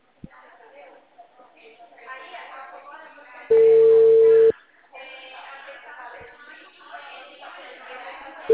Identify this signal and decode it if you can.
Ringback tone (China, 450 Hz), with speech.
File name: out- -2013- - - .43905